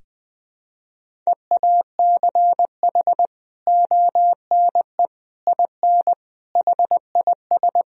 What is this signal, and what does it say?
Morse code, heard as a keyed tone, about 20 wpm, 700 Hz. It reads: EACH ONE IN HIS
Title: Sentences from Top 1000 Words - Review Part 1 - 20wpm